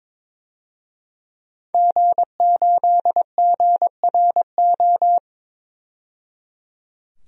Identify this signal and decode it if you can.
Morse code, heard as a keyed tone, about 22 wpm, 700 Hz. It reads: G8GRO
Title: DXCC-100 Call Signs - 5 Characters - 22wpm